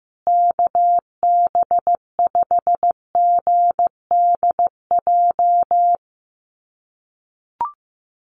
Morse code, heard as a keyed tone, about 15 wpm, 700 Hz. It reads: KB5GDJ
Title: USA Call Signs Repeated 2x 15wpm